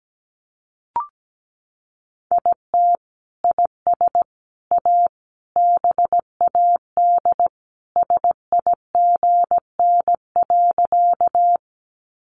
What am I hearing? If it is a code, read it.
Morse code, heard as a keyed tone, about 17 wpm, 700 Hz. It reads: IT IS A BAD SIGN.
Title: Sentences from Top 1000 Words 17wpm